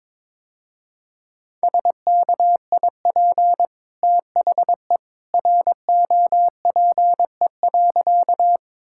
Morse code, heard as a keyed tone, about 22 wpm, 700 Hz. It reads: SKIP THE ROPE.
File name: practice / 22wpm / Easy Sentences with Hints - 22wpm